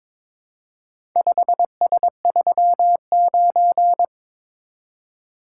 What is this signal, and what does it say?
Morse code, heard as a keyed tone, about 22 wpm, 700 Hz. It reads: 5S39
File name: practice / 22wpm / Four Letter-Number Combinations - 22wpm